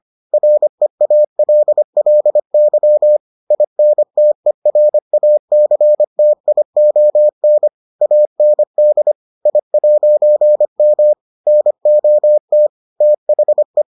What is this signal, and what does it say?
Morse code, heard as a keyed tone, about 25 wpm, 600 Hz. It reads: REALLY INTERACTION AND I'M NOT THE